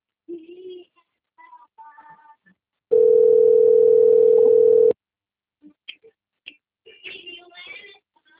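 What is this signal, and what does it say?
Ringback tone (North America), 440 + 480 Hz.
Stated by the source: Telephone call recording from Atibaia - SP.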